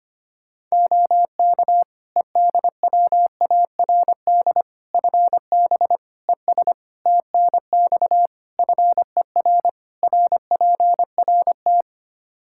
Morse code, heard as a keyed tone, about 25 wpm, 700 Hz. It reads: OK EDWARD FB ES TNX FER RPRT
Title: Practice QSO - Exchange 2 - Info - 25wpm